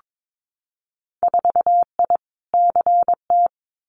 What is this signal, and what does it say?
Morse code, heard as a keyed tone, about 22 wpm, 700 Hz. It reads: 4I CT
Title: Winter Field Day Exchanges Encore 22wpm